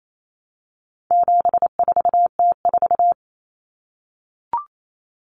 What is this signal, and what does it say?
Morse code, heard as a keyed tone, about 28 wpm, 700 Hz. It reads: 74T4
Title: Common Cut Numbers - 4-Digit 28wpm